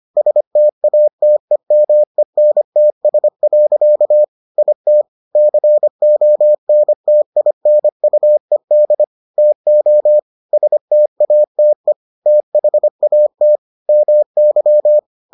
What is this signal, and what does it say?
Morse code, heard as a keyed tone, about 25 wpm, 600 Hz. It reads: STATEMENTS. IT CONTINUED TO STATE THAT MY